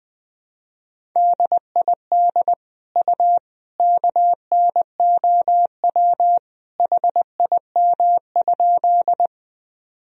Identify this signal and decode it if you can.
Morse code, heard as a keyed tone, about 21 wpm, 700 Hz. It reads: DID U KNOW HIM?
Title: Sentences from Top 100 Words and Abbreviations 20wpm